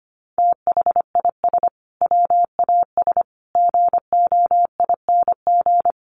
Morse code, heard as a keyed tone, about 25 wpm, 700 Hz. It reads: THIS WAS GOING
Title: Sets of 5 Words Encore2 25wpm